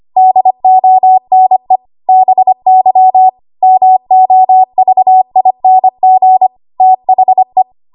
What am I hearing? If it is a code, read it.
Morse code, heard as a keyed tone, about 25 wpm, 750 Hz. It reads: DONE BY MOVING THE
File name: Archive / 25 WPM / 25WPM